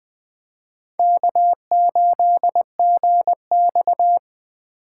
Morse code, heard as a keyed tone, about 20 wpm, 700 Hz. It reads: K8GX
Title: USA Call Signs - 4 Characters - 2x Character Spacing - 20wpm